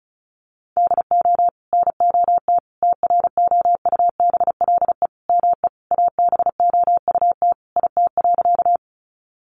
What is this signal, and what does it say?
Morse code, heard as a keyed tone, about 34 wpm, 700 Hz. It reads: DO NOT TROUBLE ME ABOUT IT.
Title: Sentences from Top 1200 Words 35wpm